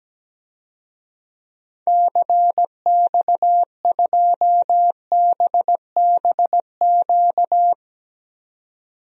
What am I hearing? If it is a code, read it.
Morse code, heard as a keyed tone, about 17 wpm, 700 Hz. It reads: CX2BBQ
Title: DXCC-100 Call Signs - 6 Characters - 17wpm